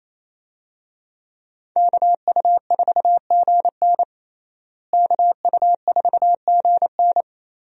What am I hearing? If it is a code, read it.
Morse code, heard as a keyed tone, about 28 wpm, 700 Hz. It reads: KU4GN KU4GN
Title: USA Call Signs - 5 Characters - Repeated 3x - 28wpm